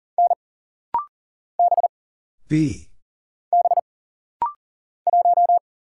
Morse code, heard as a keyed tone, about 40 wpm, 700 Hz. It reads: N B B 1
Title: Single Letter-Number - Rapid-Fire 40wpm